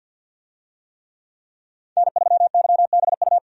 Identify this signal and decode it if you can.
Morse code, heard as a keyed tone, about 50 wpm, 700 Hz. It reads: N3YBU
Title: USA Call Signs - 5 Characters - 50wpm